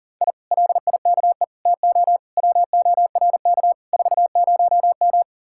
Morse code, heard as a keyed tone, about 50 wpm, 700 Hz. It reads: I LIKE TO WORK 40M
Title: Speed-Racing Common QSO Phrases - 40wpm